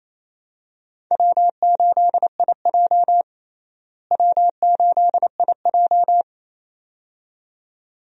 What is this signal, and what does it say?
Morse code, heard as a keyed tone, about 28 wpm, 700 Hz. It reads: W8IJ W8IJ
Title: USA Call Signs - 4 Characters - Repeated 2x - 28wpm